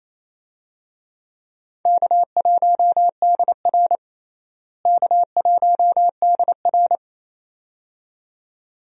Morse code, heard as a keyed tone, about 28 wpm, 700 Hz. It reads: K1DR K1DR
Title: USA Call Signs - 4 Characters - Repeated 2x - 28wpm